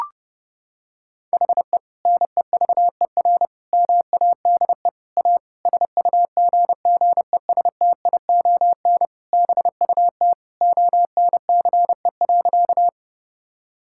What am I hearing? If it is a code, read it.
Morse code, heard as a keyed tone, about 30 wpm, 700 Hz. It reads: HE NEVER MADE A SUGGESTION BUT ONCE.
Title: Sentences from Top 1600 Words 30wpm